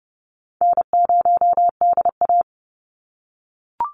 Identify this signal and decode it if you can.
Morse code, heard as a keyed tone, about 30 wpm, 700 Hz. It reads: N0DA
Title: NA SOTA Chaser Callsigns 30wpm